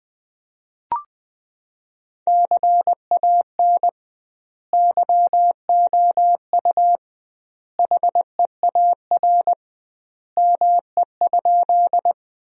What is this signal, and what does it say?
Morse code, heard as a keyed tone, about 20 wpm, 700 Hz. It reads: CAN YOU HEAR ME?
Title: 2x Word-Spacing - Sentences from Top 300 Words - 20wpm